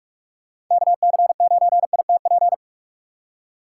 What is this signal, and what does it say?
Morse code, heard as a keyed tone, about 43 wpm, 700 Hz. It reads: KC9ITP
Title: USA Call Signs - 6x Character Spacing - 45wpm